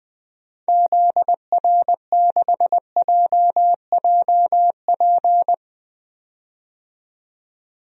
Morse code, heard as a keyed tone, about 21 wpm, 700 Hz. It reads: ZR6JJP
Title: DXCC-100 Call Signs - 6 Characters - 20wpm